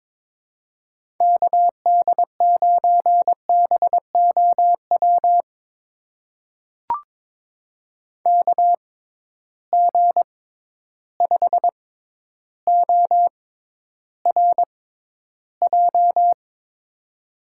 Morse code, heard as a keyed tone, about 22 wpm, spaced as for 11 wpm, 700 Hz. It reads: KD9BOW KG5ORJ
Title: USA Call Signs - 6 Characters - 6x Character Spacing - 22wpm